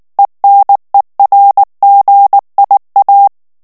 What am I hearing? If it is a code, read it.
Morse code, heard as a keyed tone, about 19 wpm, 800 Hz. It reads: ENERGIA
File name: ESPAGNOL TEXTE EN CLAIR 2 VITESSE 18 MOTS MINUTES